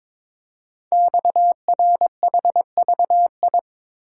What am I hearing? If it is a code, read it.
Morse code, heard as a keyed tone, about 22 wpm, 700 Hz. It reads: XRHVI
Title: Code Groups - 5 Letters - 22wpm